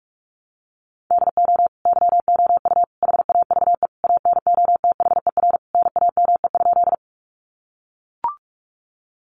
Morse code, heard as a keyed tone, about 44 wpm, 700 Hz. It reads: DO YOU HAVE ANOTHER NAME?
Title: Sentences from Top 600 Words 45wpm